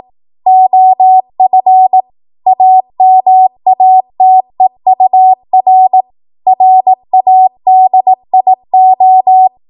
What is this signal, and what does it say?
Morse code, heard as a keyed tone, about 18 wpm, 750 Hz. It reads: OF AMATEUR RADIO
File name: Morse / 18WPM